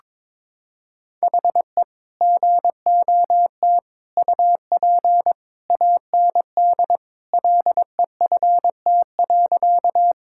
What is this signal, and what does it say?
Morse code, heard as a keyed tone, about 22 wpm, 700 Hz. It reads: HE GOT UP AND LEFT.
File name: practice / 22wpm / Sentences with 2-Word Phrases - 22wpm